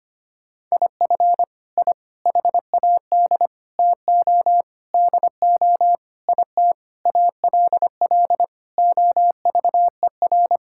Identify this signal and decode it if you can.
Morse code, heard as a keyed tone, about 25 wpm, 700 Hz. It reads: IF I HAD TO DO IT ALL OVER
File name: Sets of 9 Words - 25wpm